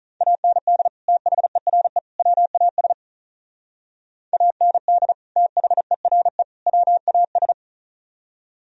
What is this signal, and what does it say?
Morse code, heard as a keyed tone, about 44 wpm, 700 Hz. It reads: AND THERE WAS AND THERE WAS
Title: Speed-Racing - Sets of 3 Words 35wpm